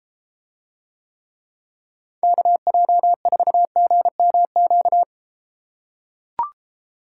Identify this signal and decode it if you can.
Morse code, heard as a keyed tone, about 32 wpm, 700 Hz. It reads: KJ4GMQ